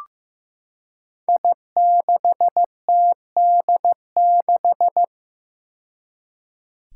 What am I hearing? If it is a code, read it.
Morse code, heard as a keyed tone, about 15 wpm, 700 Hz. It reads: I6TD6
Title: Five Challenging Letter-Number Combinations 15wpm